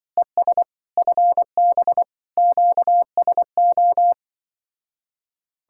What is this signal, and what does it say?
Morse code, heard as a keyed tone, about 23 wpm, 700 Hz. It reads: ES FB QSO
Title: Speed-Racing Common QSO Phrases Encore - 20wpm